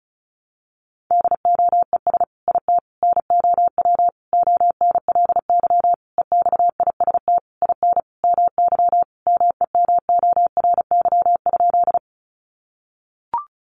Morse code, heard as a keyed tone, about 34 wpm, 700 Hz. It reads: DOES IT NOW ONLY EXIST IN MY MEMORY?
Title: Sentences from Top 1300 Words 35wpm